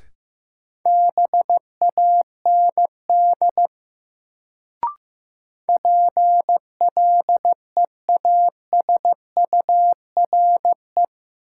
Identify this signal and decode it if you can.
Morse code, heard as a keyed tone, about 15 wpm, 700 Hz. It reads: BAND PLEASURE
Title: Sentences from Top 1600 Words - New Word Drill - 15wpm